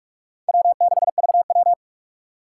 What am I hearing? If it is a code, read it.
Morse code, heard as a keyed tone, about 46 wpm, 700 Hz. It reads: W6VW
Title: USA Call Signs - 3x Character Spacing - 45wpm